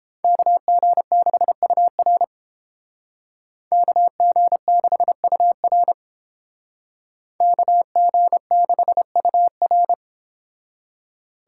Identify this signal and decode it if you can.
Morse code, heard as a keyed tone, about 31 wpm, 700 Hz. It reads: KG6UR KG6UR KG6UR